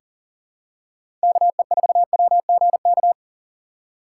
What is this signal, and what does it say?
Morse code, heard as a keyed tone, about 40 wpm, 700 Hz. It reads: KE4WGK
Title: USA Call Signs - 3x Character Spacing - 40wpm